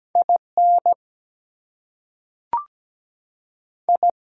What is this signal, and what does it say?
Morse code, heard as a keyed tone, about 17 wpm, 700 Hz. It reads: IN I
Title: Two Letter Words - 17wpm